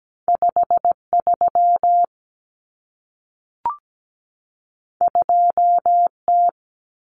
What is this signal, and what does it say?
Morse code, heard as a keyed tone, about 17 wpm, 700 Hz. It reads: 53 2T
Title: Common Cut Numbers - 2-Digit 17wpm